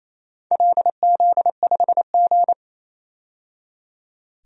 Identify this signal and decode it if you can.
Morse code, heard as a keyed tone, about 28 wpm, 700 Hz. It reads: LZ5G
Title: DXCC-100 Call Signs - 4 Characters - 28wpm